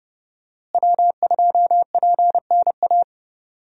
Morse code, heard as a keyed tone, about 30 wpm, 700 Hz. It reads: W2PNA
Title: USA Call Signs - 5 Characters - Repeated 2x - 30wpm